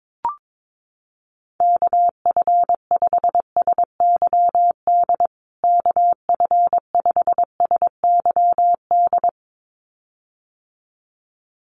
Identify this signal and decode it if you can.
Morse code, heard as a keyed tone, about 22 wpm, 700 Hz. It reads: KF5SYD KF5SYD
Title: USA Call Signs Repeated 2x 22wpm